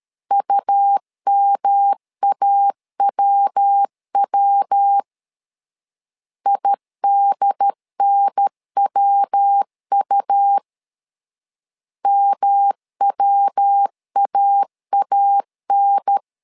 Morse code, heard as a keyed tone, about 13 wpm, 800 Hz. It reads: UMAWW IDNWU MWAAN
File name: LEZIONI / 3 MWRO / 3 MW 12wpm 1633